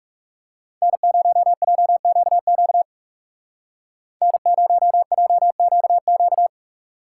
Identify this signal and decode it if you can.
Morse code, heard as a keyed tone, about 36 wpm, 700 Hz. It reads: N0JQQ N0JQQ
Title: Speed Racing - USA 1x3 Call Signs 30wpm